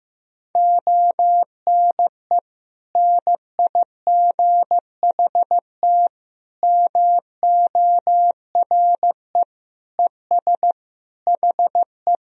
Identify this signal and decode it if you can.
Morse code, heard as a keyed tone, about 15 wpm, 700 Hz. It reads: ONE NIGHT MORE ES HE
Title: Sentences from Top 300 Words and Abbreviations 15wpm